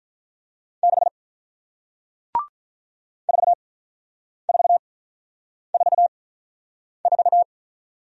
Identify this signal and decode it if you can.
Morse code, heard as a keyed tone, about 48 wpm, spaced as for 9 wpm, 700 Hz. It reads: 6 4444